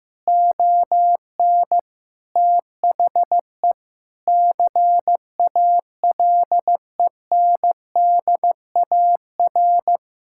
Morse code, heard as a keyed tone, about 15 wpm, 700 Hz. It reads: ON THE CALENDAR
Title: Sentences from Top 1600 Words 15wpm